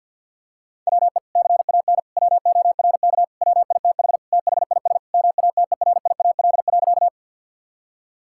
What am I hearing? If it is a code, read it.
Morse code, heard as a keyed tone, about 50 wpm, 700 Hz. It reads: WE CAN WORK WITH THIS MATERIAL.
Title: Sentences from Top 1000 Words - Review Part 5 - 50wpm